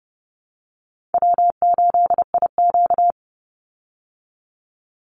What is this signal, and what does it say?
Morse code, heard as a keyed tone, about 30 wpm, 700 Hz. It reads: W8IQ
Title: Call Signs Repeated 2x - 30wpm